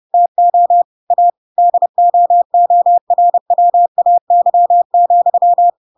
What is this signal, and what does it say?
Morse code, heard as a keyed tone, about 30 wpm, 700 Hz. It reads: TO A DOORWAY,
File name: Pinocchio / Pinocchio ch6 30 30 ik1hge